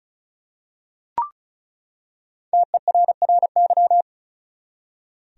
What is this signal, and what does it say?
Morse code, heard as a keyed tone, about 36 wpm, 700 Hz. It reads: TERRY
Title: USA Common Names - 35wpm